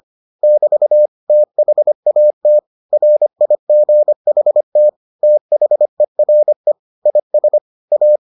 Morse code, heard as a keyed tone, about 25 wpm, 600 Hz. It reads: <BT> THAT RIGHT THERE IS A